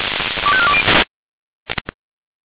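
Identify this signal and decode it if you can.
Five-tone selective call (ZVEI), heard as quick runs of five tones, tones of about 70 ms. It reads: 25420
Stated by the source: ZVEI five-tone call 25417 received by SDR, recorded by feuerwehr-orlamuende.org.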